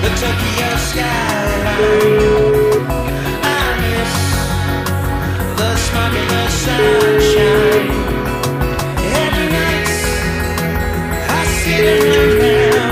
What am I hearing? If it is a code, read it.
Ringback tone (Europe, 425 Hz), with noise.